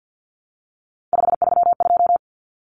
Morse code, heard as a keyed tone, about 50 wpm, 700 Hz. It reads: 532
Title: Numbers - 3-Digit 50wpm